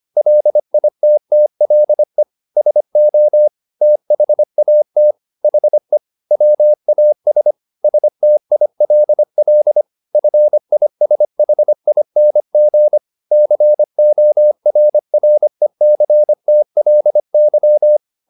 Morse code, heard as a keyed tone, about 25 wpm, 600 Hz. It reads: LITTLE SO THAT HE WAS STILL FISHING CORRECTLY
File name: the old man and the sea part 1 25wpm